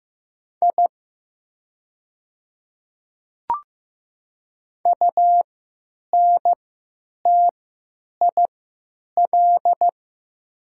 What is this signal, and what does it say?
Morse code, heard as a keyed tone, about 15 wpm, spaced as for 9 wpm, 700 Hz. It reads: I UNTIL
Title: 3x Character Spacing - Top 300 Words - 15wpm